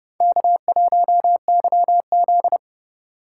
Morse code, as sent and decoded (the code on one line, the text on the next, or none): -.- .---- -.-- --..
K1YZ